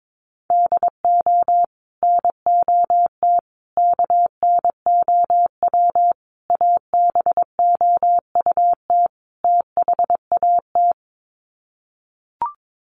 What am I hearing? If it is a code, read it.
Morse code, heard as a keyed tone, about 22 wpm, 700 Hz. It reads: DO NOT KNOW ABOUT THAT
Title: Sets of 5 Words Encore2 22wpm